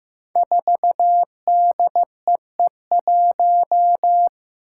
Morse code, heard as a keyed tone, about 15 wpm, 700 Hz. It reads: 4DEE1
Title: Five Challenging Letter-Number Combinations 15wpm